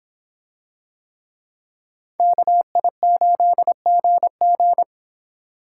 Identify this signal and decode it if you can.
Morse code, heard as a keyed tone, about 26 wpm, 700 Hz. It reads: KI8GG